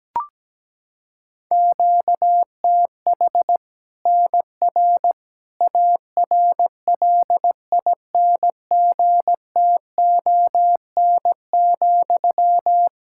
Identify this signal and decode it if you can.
Morse code, heard as a keyed tone, about 17 wpm, 700 Hz. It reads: QTH NR ARLINGTON,
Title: Practice QSO - Exchange 1 - Beginning - 17wpm